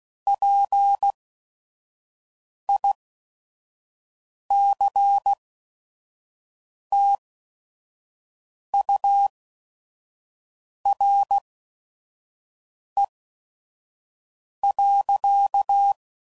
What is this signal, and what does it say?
Morse code, heard as a keyed tone, about 16 wpm, spaced as for 5 wpm, 800 Hz. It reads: PICTURE.
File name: CW Code Groups @5WPM / Technical Subjects / AND Gate